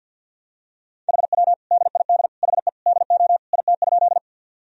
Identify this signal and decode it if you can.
Morse code, heard as a keyed tone, about 47 wpm, 700 Hz. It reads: HW DID HE DO IT?
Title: Sentences from Top 100 Words and Abbreviations 50wpm